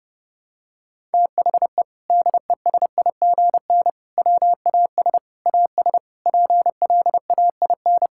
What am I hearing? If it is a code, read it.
Morse code, heard as a keyed tone, about 30 wpm, 700 Hz. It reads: THE DESIGN WAS AS PLAIN
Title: Sentences from Top 1000 Words - Review Part 5 - 30wpm